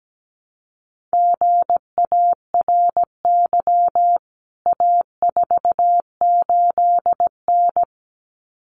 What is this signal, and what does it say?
Morse code, heard as a keyed tone, about 17 wpm, 700 Hz. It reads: GARY A48N
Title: CWT Exchange 17wpm